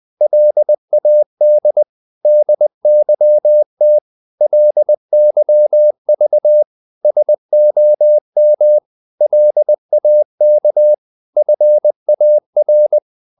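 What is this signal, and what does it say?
Morse code, heard as a keyed tone, about 20 wpm, 600 Hz. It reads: LAD DYT LYV SOM LAK FAR
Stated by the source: Morse code practice of short Danish words at 20 wpm, from CW med Gnister.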